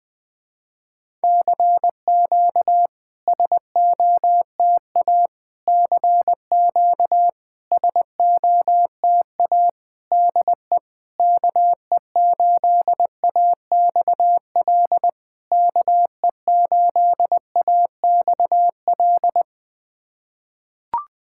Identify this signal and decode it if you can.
Morse code, heard as a keyed tone, about 20 wpm, 700 Hz. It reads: CQ SOTA CQ SOTA DE KE8AXL KE8AXL
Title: SOTA QSO Practice 20wpm